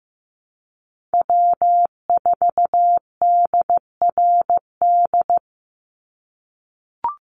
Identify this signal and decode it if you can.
Morse code, heard as a keyed tone, about 15 wpm, 700 Hz. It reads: W4DRD
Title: USA Call Signs 15wpm